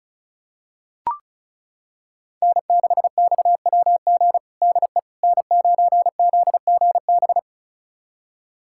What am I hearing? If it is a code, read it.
Morse code, heard as a keyed tone, about 36 wpm, 700 Hz. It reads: N6XWG DE N9ZGB
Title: USA Callsign DE Callsign - 5 Characters - 35wpm